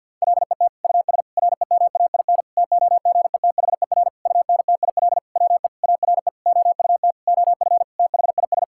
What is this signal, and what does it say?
Morse code, heard as a keyed tone, about 50 wpm, 700 Hz. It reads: LET US REMAIN TOGETHER UNTIL WE ARE OUT OF THIS
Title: Sentences from Top 1600 Words 50wpm